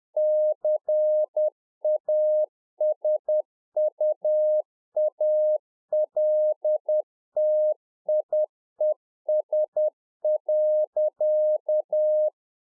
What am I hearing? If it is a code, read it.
Morse code, heard as a keyed tone, about 10 wpm, 610 Hz. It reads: CASUALTIES.